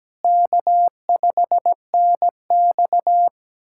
Morse code, heard as a keyed tone, about 17 wpm, 700 Hz. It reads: K5NX
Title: USA Callsign DE Callsign - 4 Characters - 17wpm